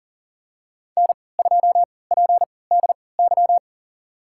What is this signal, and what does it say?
Morse code, heard as a keyed tone, about 40 wpm, spaced as for 23 wpm, 700 Hz. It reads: N2PDY
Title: USA Call Signs - 5 Characters - 3x Character Spacing - 40wpm